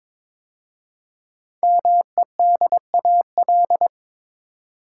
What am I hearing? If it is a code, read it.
Morse code, heard as a keyed tone, about 22 wpm, 700 Hz. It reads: MEDAL
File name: 5 Letter Words - First 3 Letters Given - 22wpm